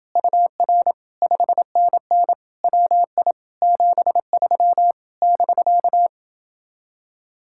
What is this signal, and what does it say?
Morse code, heard as a keyed tone, about 28 wpm, 700 Hz. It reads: UR 5NN WI 73 <BK>